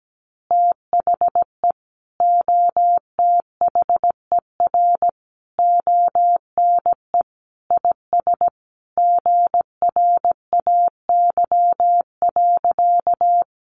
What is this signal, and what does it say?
Morse code, heard as a keyed tone, about 17 wpm, 700 Hz. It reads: THE OTHER ONE IS GRAY.
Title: Additional Easy Phrases 17wpm